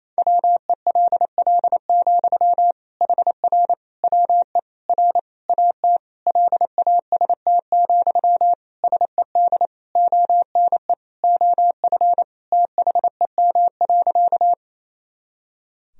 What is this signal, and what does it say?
Morse code, heard as a keyed tone, about 28 wpm, 700 Hz. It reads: WELL, HR WE R AT LAST, SED ONE OF THEM.
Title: Sentences from Top 300 Words and Abbreviations 28wpm